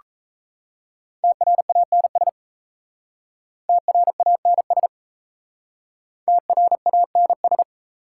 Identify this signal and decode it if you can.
Morse code, heard as a keyed tone, about 37 wpm, 690 Hz. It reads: TRANS TRANS TRANS